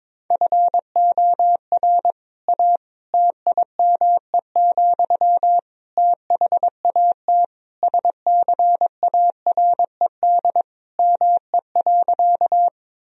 Morse code, heard as a keyed tone, about 21 wpm, 700 Hz. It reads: FOR A TIME, THAT SCARED ME.
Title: Sentences from Top 1000 Words - Review Part 4 - 22wpm